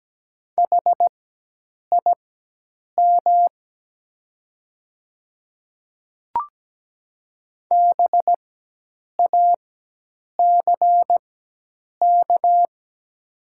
Morse code, heard as a keyed tone, about 17 wpm, spaced as for 8 wpm, 700 Hz. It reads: HIM BACK
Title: 4x Character Spacing - Top 300 Words - 17wpm